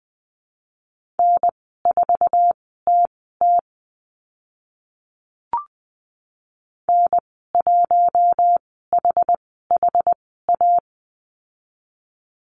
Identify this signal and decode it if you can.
Morse code, heard as a keyed tone, about 20 wpm, spaced as for 15 wpm, 700 Hz. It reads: N4TT N1HHA
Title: 082 Lesson 14g - Number 2 - Call Signs 20wpm